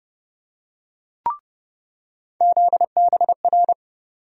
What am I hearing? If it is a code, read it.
Morse code, heard as a keyed tone, about 30 wpm, 700 Hz. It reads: ZBR
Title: USA Call Sign Postfixes - 3 Letters 30wpm